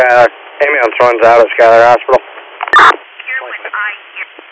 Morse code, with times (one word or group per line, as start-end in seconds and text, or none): none